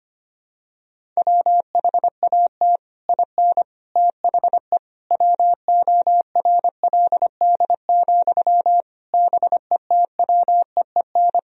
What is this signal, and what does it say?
Morse code, heard as a keyed tone, about 25 wpm, 700 Hz. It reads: WHAT IN THE WORLD, BETWEEN
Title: Sentences from Top 1000 Words - Review Part 1 - 25wpm